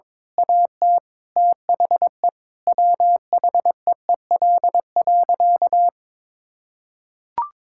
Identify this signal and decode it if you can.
Morse code, heard as a keyed tone, about 22 wpm, 700 Hz. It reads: AT THE WHEEL.
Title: Sentences from Top 1000 Words - Review Part 5 - 22wpm